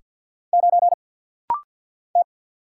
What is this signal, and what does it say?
Morse code, heard as a keyed tone, about 50 wpm, 700 Hz. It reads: 9 T
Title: Single Letter-Number Alt - Rapid-Fire 50wpm